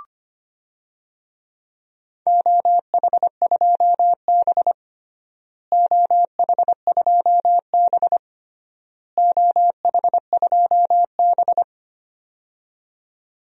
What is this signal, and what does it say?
Morse code, heard as a keyed tone, about 25 wpm, 700 Hz. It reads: OH2B OH2B OH2B
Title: DXCC-100 Call Signs - 4 Characters - Repeated 3x - 25wpm